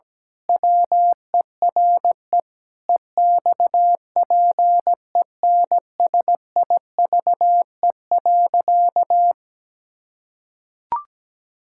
Morse code, heard as a keyed tone, about 17 wpm, 700 Hz. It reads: WERE EXPENSIVE.
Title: Pangram Challenge v2 - 17wpm